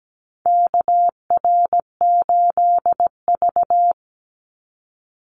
Morse code, heard as a keyed tone, about 17 wpm, 700 Hz. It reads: KR8V
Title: Call Signs - 17wpm